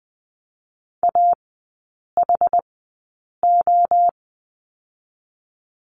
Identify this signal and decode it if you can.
Morse code, heard as a keyed tone, about 20 wpm, spaced as for 8 wpm, 700 Hz. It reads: AHO
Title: 087 Lesson 15e - Number 5 - Three Characters 20wpm